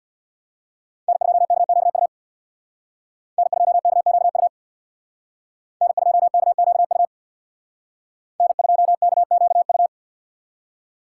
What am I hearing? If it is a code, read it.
Morse code, heard as a keyed tone, about 66 wpm, 700 Hz. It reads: N2KQU N2KQU N2KQU N2KQU